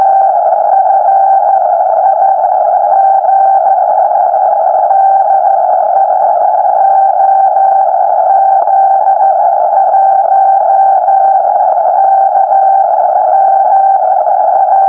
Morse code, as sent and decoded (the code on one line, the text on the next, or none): none